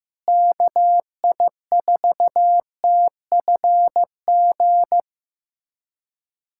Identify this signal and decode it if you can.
Morse code, heard as a keyed tone, about 16 wpm, 700 Hz. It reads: KI4TFG
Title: USA Callsign DE Callsign - 6 Characters - 15wpm